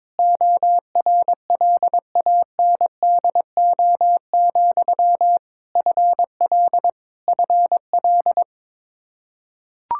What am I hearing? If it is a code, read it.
Morse code, heard as a keyed tone, about 22 wpm, 700 Hz. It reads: ORLANDO, FL FL
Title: Practice QSO - Exchange 1 - Beginning - 22wpm